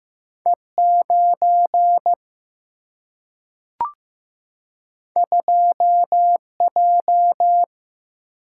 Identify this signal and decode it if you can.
Morse code, heard as a keyed tone, about 15 wpm, 700 Hz. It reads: E9 2J
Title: Single-Letter Single-Number 15wpm